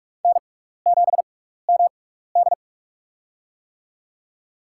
Morse code, heard as a keyed tone, about 46 wpm, spaced as for 16 wpm, 700 Hz. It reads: N7MD
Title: USA Call Signs - 4 Characters - 6x Character Spacing - 45wpm